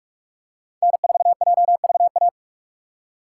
Morse code, heard as a keyed tone, about 44 wpm, 690 Hz. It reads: N4JVA
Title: USA Call Signs - 5 Characters - 2x Character Spacing - 45wpm